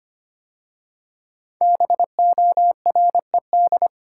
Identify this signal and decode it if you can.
Morse code, heard as a keyed tone, about 25 wpm, 700 Hz. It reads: BORED
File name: 5 Letter Words - First 3 Letters Given - 25wpm